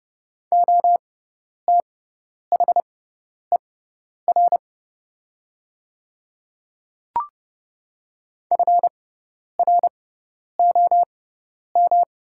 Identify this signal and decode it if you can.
Morse code, heard as a keyed tone, about 30 wpm, spaced as for 10 wpm, 700 Hz. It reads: OTHER FROM
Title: 6x Character Spacing - Top 200 Words - 30wpm